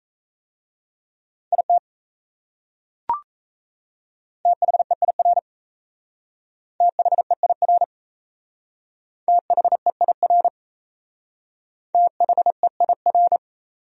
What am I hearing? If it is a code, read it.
Morse code, heard as a keyed tone, about 35 wpm, 700 Hz. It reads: IT THEIR THEIR THEIR THEIR